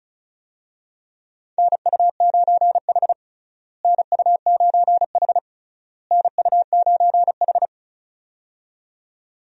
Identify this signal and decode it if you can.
Morse code, heard as a keyed tone, about 34 wpm, 700 Hz. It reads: NU9H NU9H NU9H
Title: USA Call Signs - 4 Characters - Repeated 3x - 35wpm